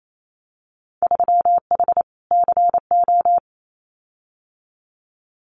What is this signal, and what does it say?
Morse code, heard as a keyed tone, about 28 wpm, 700 Hz. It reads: 3H CO
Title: Winter Field Day Exchanges 28wpm